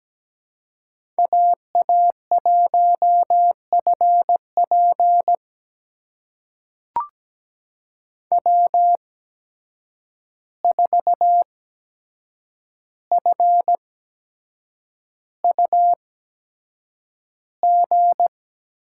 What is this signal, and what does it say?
Morse code, heard as a keyed tone, about 17 wpm, spaced as for 7 wpm, 700 Hz. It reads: AA1FP W4FUG
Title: USA Call Signs - 5 Characters - 8x Character Spacing - 17wpm